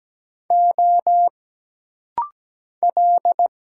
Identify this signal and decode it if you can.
Morse code, heard as a keyed tone, about 17 wpm, 700 Hz. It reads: O L
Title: Single Letters - Rapid-Fire 17wpm